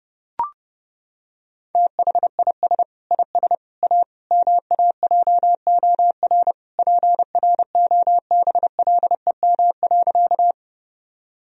Morse code, heard as a keyed tone, about 32 wpm, 700 Hz. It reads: THIS IS A MAJOR PROBLEM.
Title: Sentences from Top 1000 Words - Review Part 4 - 30wpm